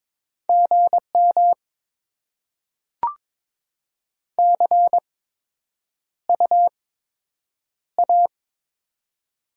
Morse code, heard as a keyed tone, about 22 wpm, 700 Hz. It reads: G M C U A
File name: Top 20 QSO Elements - 8x Character Spacing - 22wpm